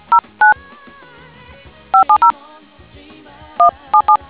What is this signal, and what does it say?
DTMF, heard as telephone touch tones, tones of about 100 ms. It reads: 09 570 177